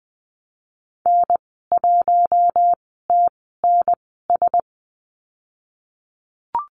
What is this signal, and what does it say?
Morse code, heard as a keyed tone, about 20 wpm, spaced as for 15 wpm, 700 Hz. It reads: N1TNS
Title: 069 Lesson 12g - Letter d - Call Signs 20wpm